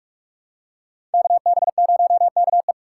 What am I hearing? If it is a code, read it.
Morse code, heard as a keyed tone, about 46 wpm, 700 Hz. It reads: KB0KE
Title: USA Call Signs - 5 Characters - 3x Character Spacing - 45wpm